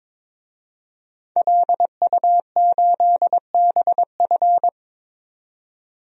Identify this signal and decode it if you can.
Morse code, heard as a keyed tone, about 22 wpm, 700 Hz. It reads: LU8BF
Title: DXCC-100 Call Signs - 5 Characters - 22wpm